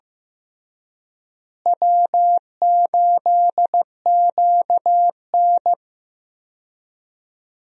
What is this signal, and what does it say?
Morse code, heard as a keyed tone, about 15 wpm, 700 Hz. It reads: W8QN
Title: USA Call Signs - 4 Characters - Repeated 3x - 15wpm